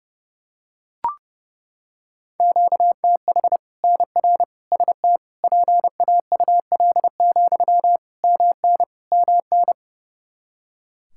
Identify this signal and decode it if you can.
Morse code, heard as a keyed tone, about 30 wpm, 700 Hz. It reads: QTH NR ST PAUL, MN MN
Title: Practice QSO - Exchange 1 - Beginning - 30wpm